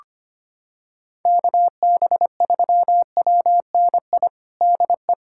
Morse code, heard as a keyed tone, about 25 wpm, 700 Hz. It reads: KB3WNI DE
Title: USA Callsign DE Callsign 25wpm